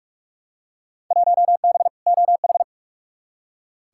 Morse code, heard as a keyed tone, about 44 wpm, 700 Hz. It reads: 1B OH
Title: Field Day Exchanges - 45wpm